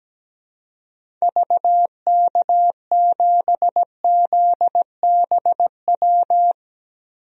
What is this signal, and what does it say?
Morse code, heard as a keyed tone, about 17 wpm, 700 Hz. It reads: VK7ZBW
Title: DXCC-100 Call Signs - 6 Characters - Repeated 2x - 17wpm